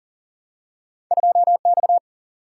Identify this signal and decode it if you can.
Morse code, heard as a keyed tone, about 39 wpm, 700 Hz. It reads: 2 X
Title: Single-Letter Single-Number 40wpm